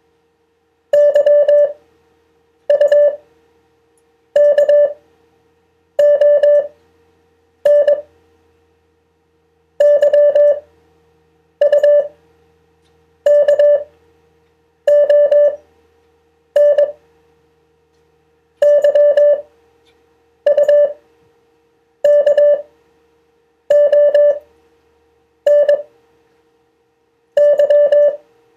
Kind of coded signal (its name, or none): Morse code